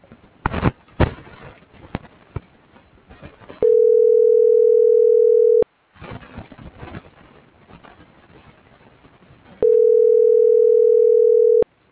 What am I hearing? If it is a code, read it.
Ringback tone (North America), 440 + 480 Hz.